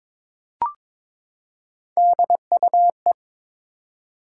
Morse code, heard as a keyed tone, about 22 wpm, 700 Hz. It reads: DUE
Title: Three Letter Words - 22wpm